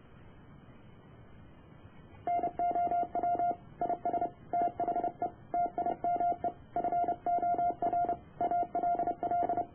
Morse code, heard as a keyed tone, about 30 wpm, 700 Hz. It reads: NOW IS THE TIME FOR ALL